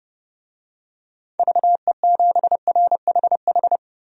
Morse code, heard as a keyed tone, about 32 wpm, 700 Hz. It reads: VE7RHH
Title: DXCC-100 Call Signs - 6 Characters - 30wpm